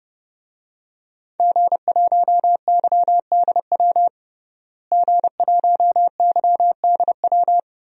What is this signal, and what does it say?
Morse code, heard as a keyed tone, about 30 wpm, 700 Hz. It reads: G1YDW G1YDW
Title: DXCC-100 Call Signs - 5 Characters - Repeated 2x - 30wpm